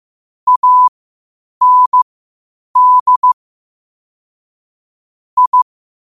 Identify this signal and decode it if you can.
Morse code, heard as a keyed tone, about 15 wpm, spaced as for 8 wpm, 1000 Hz. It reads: AND I
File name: qotdfiles / 8wpm